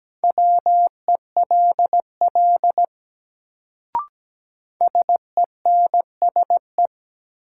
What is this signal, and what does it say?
Morse code, heard as a keyed tone, about 17 wpm, 700 Hz. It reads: WELL SENSE
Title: Sets of 2 Words - New Word Drill - 17wpm